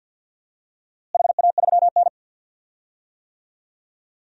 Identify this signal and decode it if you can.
Morse code, heard as a keyed tone, about 51 wpm, 700 Hz. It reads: HA3N
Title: DXCC-100 Call Signs - 4 Characters - Repeated 3x - 50wpm